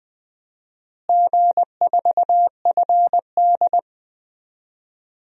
Morse code, heard as a keyed tone, about 20 wpm, 700 Hz. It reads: G4FD
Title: DXCC-100 Call Signs - 4 Characters - Repeated 3x - 20wpm